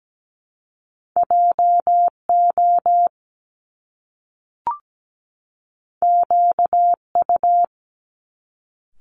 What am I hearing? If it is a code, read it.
Morse code, heard as a keyed tone, about 17 wpm, 700 Hz. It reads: JO QU
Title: Top Two Letter Combinations in Words 17wpm